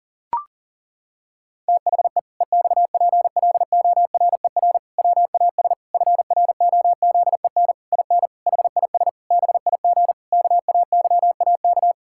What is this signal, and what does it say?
Morse code, heard as a keyed tone, about 40 wpm, 700 Hz. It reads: THE EXPLORER WAS FROZEN IN HIS BIG KAYAK
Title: Pangram Challenge v2 - 40wpm